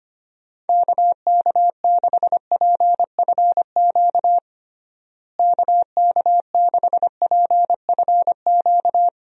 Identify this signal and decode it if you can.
Morse code, heard as a keyed tone, about 25 wpm, 700 Hz. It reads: KK6PFQ KK6PFQ
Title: USA Call Signs - 6 Characters - Repeated 3x - 25wpm